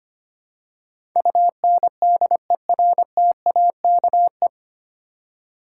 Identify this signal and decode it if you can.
Morse code, heard as a keyed tone, about 25 wpm, 700 Hz. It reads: UNDERTAKE
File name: practice / 25wpm / 9 Letter Words - First 3 Letters Given - 25wpm